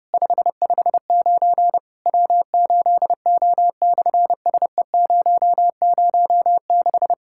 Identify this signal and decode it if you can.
Morse code, heard as a keyed tone, about 26 wpm, 700 Hz. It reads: 559 W8O/SE006
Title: SOTA QSO Practice 30wpm